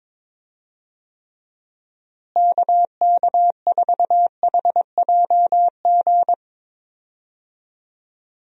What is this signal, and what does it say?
Morse code, heard as a keyed tone, about 22 wpm, 700 Hz. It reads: KK4HJG
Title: USA Call Signs - 6 Characters - Repeated 3x - 22wpm